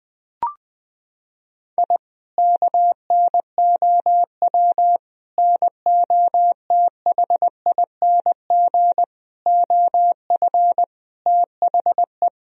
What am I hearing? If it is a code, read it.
Morse code, heard as a keyed tone, about 21 wpm, 700 Hz. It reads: I KNOW NOTHING OF THE
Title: Sentences from Top 1000 Words - Review Part 3 - 20wpm